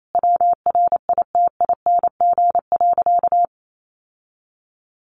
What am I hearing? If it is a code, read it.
Morse code, heard as a keyed tone, about 28 wpm, 700 Hz. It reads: WRITING.
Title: Sentences from Top 800 Words 28wpm